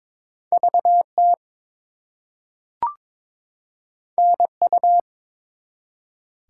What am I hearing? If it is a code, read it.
Morse code, heard as a keyed tone, about 22 wpm, 700 Hz. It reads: VT NU
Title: US States and Canadian Provinces 22wpm